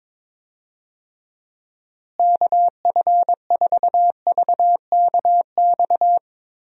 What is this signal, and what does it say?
Morse code, heard as a keyed tone, about 23 wpm, 700 Hz. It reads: KF4VKX
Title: USA Call Signs - 6 Characters - Repeated 2x - 22wpm